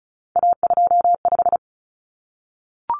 Morse code, heard as a keyed tone, about 34 wpm, 700 Hz. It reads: A25
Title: Common Cut Numbers - 3-Digit 35wpm